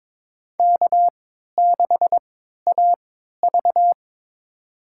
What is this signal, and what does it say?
Morse code, heard as a keyed tone, about 22 wpm, spaced as for 13 wpm, 700 Hz. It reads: K6AV
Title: USA Call Signs - 4 Characters - 3x Character Spacing - 22wpm